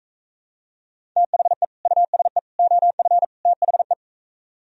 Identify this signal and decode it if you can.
Morse code, heard as a keyed tone, about 44 wpm, 700 Hz. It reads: THE USE OF THE